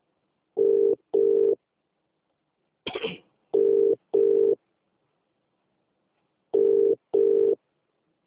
Ringback tone (UK), 400 + 450 Hz.